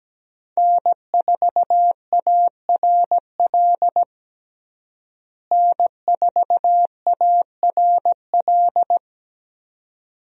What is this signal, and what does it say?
Morse code, heard as a keyed tone, about 17 wpm, 700 Hz. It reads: N4ARL N4ARL
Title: USA Call Signs - 5 Characters - Repeated 2x - 17wpm